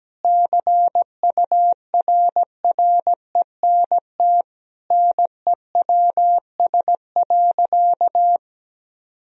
Morse code, heard as a keyed tone, about 17 wpm, 700 Hz. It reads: CURRENT NEWS.
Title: Sentences from Top 1000 Words - Review Part 4 - 17wpm